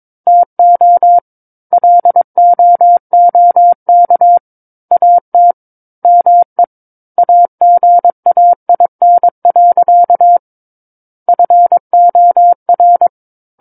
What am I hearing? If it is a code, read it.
Morse code, heard as a keyed tone, about 22 wpm, 700 Hz. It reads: TO LOOK AT ME AGAIN. FOR
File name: Book One / 13-Chapter 13 - 22 WPM